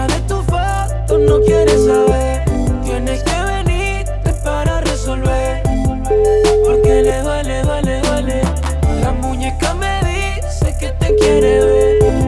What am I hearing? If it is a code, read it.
Ringback tone (Europe, 425 Hz), with noise.